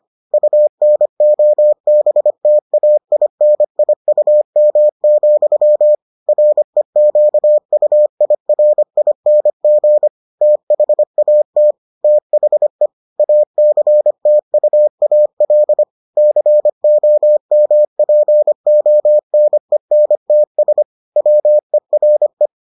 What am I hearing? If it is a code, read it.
Morse code, heard as a keyed tone, about 25 wpm, 600 Hz. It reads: UNOBTAINIUM, REQUIRING THAT THE ACTUAL COMPONENTS WERE